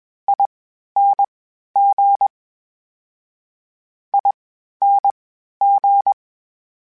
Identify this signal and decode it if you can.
Morse code, heard as a keyed tone, about 21 wpm, spaced as for 12 wpm, 800 Hz. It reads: ING ING
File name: General / General-mp3s / Suffixes sp @20